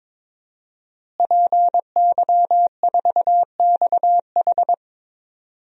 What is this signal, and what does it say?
Morse code, heard as a keyed tone, about 22 wpm, 700 Hz. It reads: PY4XH
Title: DXCC-100 Call Signs - 5 Characters - Repeated 3x - 22wpm